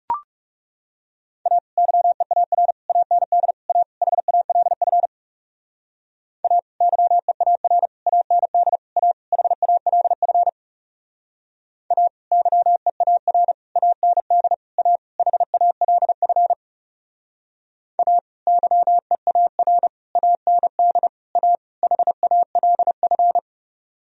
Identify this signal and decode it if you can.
Morse code, heard as a keyed tone, about 37 wpm, 700 Hz. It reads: A YEAR AND A HALF A YEAR AND A HALF A YEAR AND A HALF A YEAR AND A HALF
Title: Speed-Racing - Sets of 5 Words 30wpm